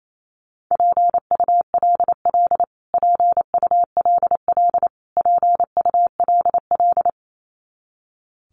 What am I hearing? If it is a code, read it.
Morse code, heard as a keyed tone, about 28 wpm, 700 Hz. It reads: PULL PULL PULL
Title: Top 500 Words Repeated 28wpm